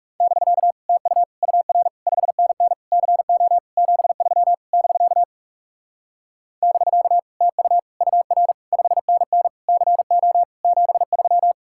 Morse code, heard as a keyed tone, about 38 wpm, 700 Hz. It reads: <BK> TU UR 5NN CO 73 <BK> <BK> TU UR 5NN CO 73